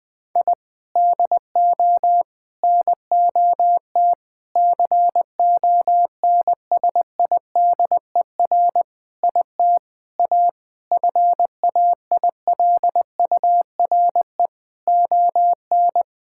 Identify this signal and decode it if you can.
Morse code, heard as a keyed tone, about 20 wpm, 700 Hz. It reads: I DO NOT CONSIDER IT A FAILURE ON
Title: Sentences from Top 1000 Words - Review Part 6 - 20wpm